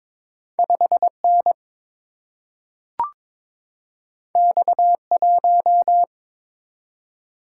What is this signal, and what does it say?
Morse code, heard as a keyed tone, about 22 wpm, 700 Hz. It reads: 5 N X 1
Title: Single-Letter Single-Number 22wpm